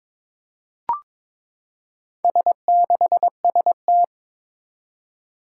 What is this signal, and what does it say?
Morse code, heard as a keyed tone, about 22 wpm, 700 Hz. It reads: S6ST
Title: Four Challenging Letter-Number Combinations 22wpm